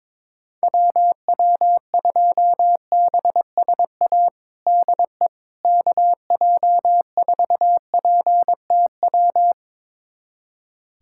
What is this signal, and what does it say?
Morse code, heard as a keyed tone, about 22 wpm, 700 Hz. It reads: WW2BSA DE KJ4PTW
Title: USA Callsign DE Callsign - 6 Characters - 22wpm